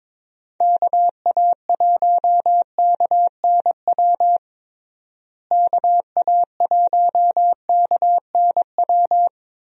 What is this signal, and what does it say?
Morse code, heard as a keyed tone, about 21 wpm, 700 Hz. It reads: KA1KNW KA1KNW
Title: USA Call Signs - 6 Characters - Repeated 2x - 22wpm